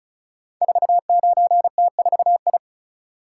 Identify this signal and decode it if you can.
Morse code, heard as a keyed tone, about 36 wpm, 700 Hz. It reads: 49T4I
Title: Code Groups - 5 Letter:Numbers - 35wpm